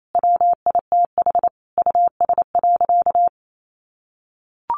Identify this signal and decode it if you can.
Morse code, heard as a keyed tone, about 28 wpm, 700 Hz. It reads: WITH US.
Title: Sentences from Top 1200 Words 28wpm